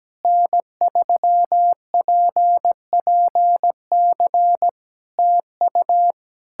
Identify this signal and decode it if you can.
Morse code, heard as a keyed tone, about 17 wpm, 700 Hz. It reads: N3PPC TU
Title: K1USN SST QSO Practice 17wpm